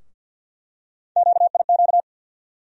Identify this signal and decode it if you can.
Morse code, heard as a keyed tone, about 52 wpm, 700 Hz. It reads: QIQ
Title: USA Call Sign Postfixes - 3 Letters 50wpm